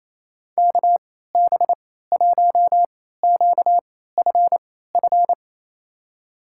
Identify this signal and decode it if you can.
Morse code, heard as a keyed tone, about 28 wpm, spaced as for 16 wpm, 700 Hz. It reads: KB1QFF
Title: USA Call Signs - 6 Characters - 3x Character Spacing - 28wpm